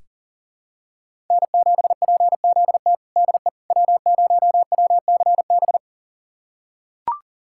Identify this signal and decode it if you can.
Morse code, heard as a keyed tone, about 40 wpm, 700 Hz. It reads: N7PZT DE W0WCB
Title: USA Callsign DE Callsign 40wpm